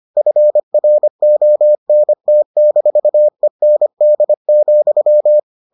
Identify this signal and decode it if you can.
Morse code, heard as a keyed tone, about 25 wpm, 600 Hz. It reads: FRONT-END,